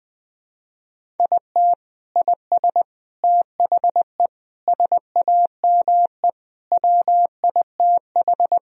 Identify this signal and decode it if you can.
Morse code, heard as a keyed tone, about 21 wpm, 700 Hz. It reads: IT IS THE SAME WITH
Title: Sentences from Top 1000 Words - Review Part 3 - 20wpm